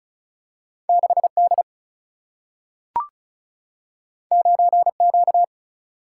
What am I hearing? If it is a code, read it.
Morse code, heard as a keyed tone, about 36 wpm, 700 Hz. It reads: 6 D 9 Q
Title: Single-Letter Single-Number 35wpm